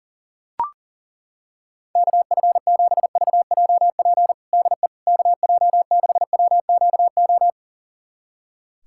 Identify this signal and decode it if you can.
Morse code, heard as a keyed tone, about 40 wpm, 700 Hz. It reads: KF7VJP DE KJ6WQO
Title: USA Callsign DE Callsign 40wpm